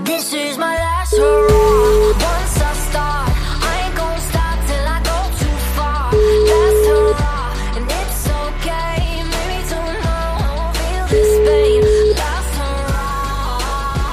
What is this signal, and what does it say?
Ringback tone (Europe, 425 Hz), with noise.